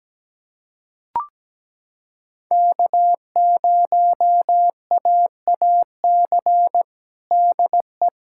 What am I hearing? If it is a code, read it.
Morse code, heard as a keyed tone, about 17 wpm, 700 Hz. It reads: K0AAC DE
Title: USA Callsign DE Callsign - 5 Characters - 17wpm